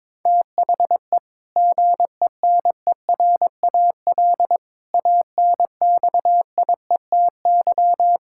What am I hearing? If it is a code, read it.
Morse code, heard as a keyed tone, about 21 wpm, 700 Hz. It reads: THE GENERAL ANXIETY
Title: Sentences from Top 1600 Words 22wpm